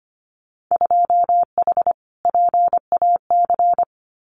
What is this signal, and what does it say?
Morse code, heard as a keyed tone, about 25 wpm, 700 Hz. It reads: 2H PAC
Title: Winter Field Day Exchanges Encore 25wpm